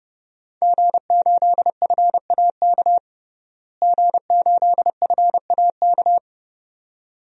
Morse code, heard as a keyed tone, about 30 wpm, 700 Hz. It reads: G8FAK G8FAK
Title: DXCC-100 Call Signs - 5 Characters - Repeated 2x - 30wpm